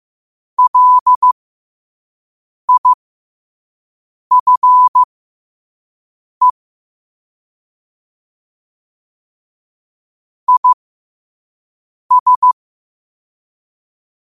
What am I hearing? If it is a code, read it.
Morse code, heard as a keyed tone, about 15 wpm, spaced as for 5 wpm, 1000 Hz. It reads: LIFE IS
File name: qotdfiles / 5wpm